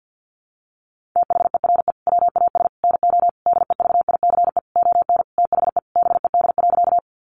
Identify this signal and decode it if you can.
Morse code, heard as a keyed tone, about 48 wpm, 700 Hz. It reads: THERE WAS NO DEVICE ON THE BED.
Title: Sentences from Top 1500 Words 50wpm